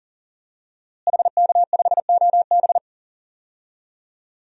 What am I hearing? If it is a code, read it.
Morse code, heard as a keyed tone, about 40 wpm, 700 Hz. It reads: HK5OB
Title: DXCC-100 Call Signs - 5 Characters - Repeated 2x - 40wpm